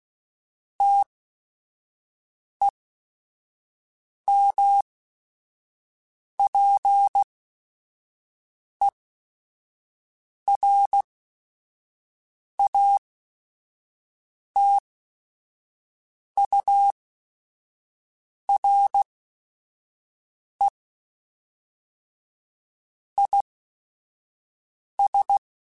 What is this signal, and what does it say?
Morse code, heard as a keyed tone, about 16 wpm, spaced as for 5 wpm, 800 Hz. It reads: TEMPERATURE IS
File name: Weather Sets / Weather Sets-mp3s / Weather 3 17 @5